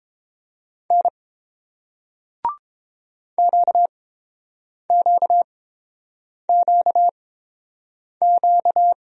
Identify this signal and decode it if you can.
Morse code, heard as a keyed tone, about 30 wpm, spaced as for 7 wpm, 700 Hz. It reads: N QQQQ